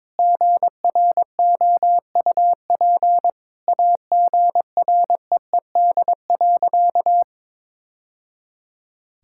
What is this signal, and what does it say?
Morse code, heard as a keyed tone, about 21 wpm, 700 Hz. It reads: GROUP AGREED.
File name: practice / 22wpm / Sentences with 3-Word Phrases - 22wpm